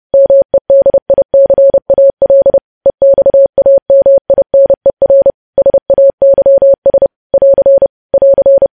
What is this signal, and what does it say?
Morse code, heard as a keyed tone, about 30 wpm, 550 Hz. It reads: MEDICAL EXAMINER SAYS <AR> <AR>
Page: news headlines in morse code at 30 wpm